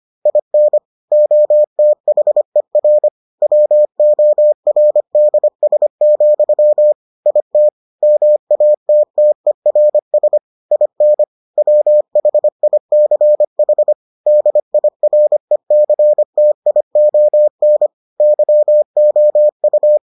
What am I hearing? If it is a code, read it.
Morse code, heard as a keyed tone, about 25 wpm, 600 Hz. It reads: IN OTHER WORDS, IT MATTERS IN WHICH DIRECTION YOU